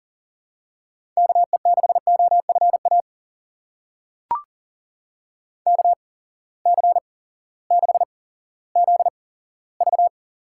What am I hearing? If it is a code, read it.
Morse code, heard as a keyed tone, about 40 wpm, spaced as for 18 wpm, 700 Hz. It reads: KE6OFA KC6ZV
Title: USA Call Signs - 6 Characters - 8x Character Spacing - 40wpm